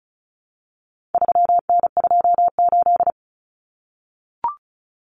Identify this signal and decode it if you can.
Morse code, heard as a keyed tone, about 34 wpm, 700 Hz. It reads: 3N28
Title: Common Cut Numbers - 4-Digit 35wpm